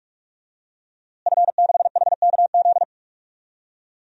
Morse code, heard as a keyed tone, about 41 wpm, 700 Hz. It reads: F6HKZ
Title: DXCC-100 Call Signs - 5 Characters - Repeated 2x - 45wpm